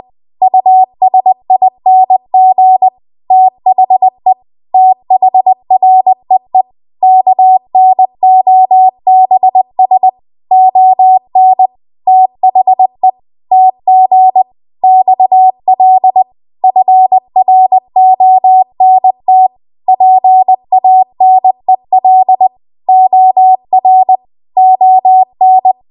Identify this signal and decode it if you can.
Morse code, heard as a keyed tone, about 20 wpm, 750 Hz. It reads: USING THE THREE KNOBS ON THE TG XL FRONT PANEL OR ON
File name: Archive / 20 WPM / 20WPM